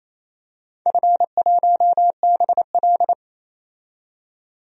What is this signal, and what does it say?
Morse code, heard as a keyed tone, about 28 wpm, 700 Hz. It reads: F1BL
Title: DXCC-100 Call Signs - 4 Characters - 28wpm